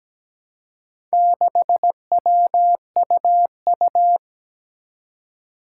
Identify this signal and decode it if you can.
Morse code, heard as a keyed tone, about 17 wpm, 700 Hz. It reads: 6WUU
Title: Four Challenging Letter-Number Combinations 17wpm